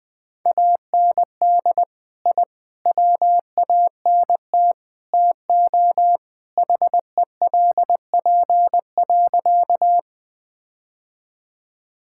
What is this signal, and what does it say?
Morse code, heard as a keyed tone, about 20 wpm, 700 Hz. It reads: AND I WANT TO HELP.
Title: Sentences with 4-Word Phrases - Spoken Beginning - 20wpm